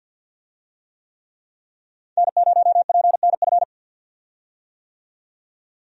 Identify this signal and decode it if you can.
Morse code, heard as a keyed tone, about 48 wpm, 700 Hz. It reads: N0PNF
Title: USA Call Signs - 5 Characters - 50wpm